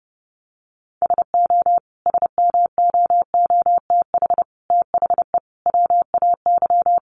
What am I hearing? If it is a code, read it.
Morse code, heard as a keyed tone, about 30 wpm, 700 Hz. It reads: SO SMOOTH THE WAY
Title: Sentences from Top 1400 Words 30wpm